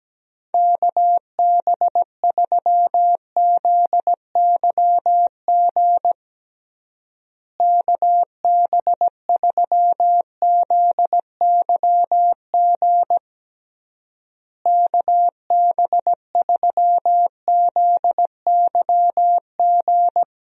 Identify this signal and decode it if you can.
Morse code, heard as a keyed tone, about 17 wpm, 700 Hz. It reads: KB3ZYG KB3ZYG KB3ZYG
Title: USA Call Signs - 6 Characters - Repeated 3x - 17wpm